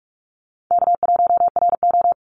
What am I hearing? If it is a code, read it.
Morse code, heard as a keyed tone, about 46 wpm, 700 Hz. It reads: K1RO
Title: Easy Call Signs - 45wpm